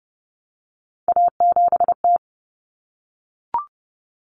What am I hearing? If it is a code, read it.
Morse code, heard as a keyed tone, about 30 wpm, 700 Hz. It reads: A7T
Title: Common Cut Numbers - 3-Digit 30wpm